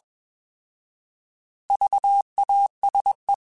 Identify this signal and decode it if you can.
Morse code, heard as a keyed tone, about 21 wpm, 800 Hz. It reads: VASE
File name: Alphabetical / Alphabetical mp3s / Vs @20